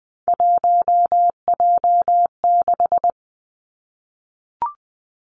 Morse code, heard as a keyed tone, about 20 wpm, 700 Hz. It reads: 1J6
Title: Three Challenging Letter-Number Combinations 20wpm